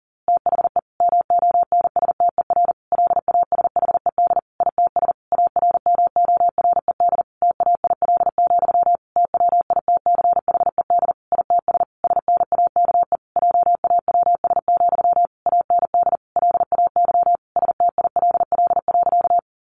Morse code, heard as a keyed tone, about 40 wpm, 700 Hz. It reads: THE MONSTER LASHED ITS ARMORED TAIL, TWITCHED ITS SNAKE JAWS, AND LAY STILL.
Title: A Sound Of Thunder 40wpm